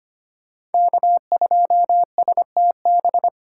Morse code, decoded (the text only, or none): K2STB